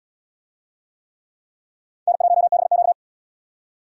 Morse code, heard as a keyed tone, about 69 wpm, 700 Hz. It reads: N1BQ